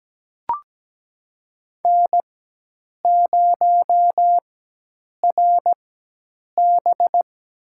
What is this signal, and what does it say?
Morse code, heard as a keyed tone, about 17 wpm, spaced as for 8 wpm, 700 Hz. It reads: N0RB
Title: USA Call Signs - 4x Character Spacing - 17wpm